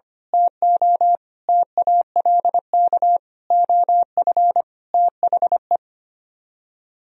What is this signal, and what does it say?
Morse code, heard as a keyed tone, about 24 wpm, 700 Hz. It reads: TO TALK OF THE
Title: Speed-Racing - Sets of 5 Words 25wpm